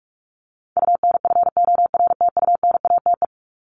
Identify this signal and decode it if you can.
Morse code, heard as a keyed tone, about 44 wpm, 700 Hz. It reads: UNFORTUNATE
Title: LY Suffix - Top 100 Words with Base Words 45wpm